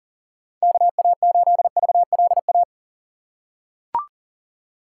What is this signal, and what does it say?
Morse code, heard as a keyed tone, about 40 wpm, 700 Hz. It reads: KA8VLA
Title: USA Call Signs - 6 Characters - 3x Character Spacing - 40wpm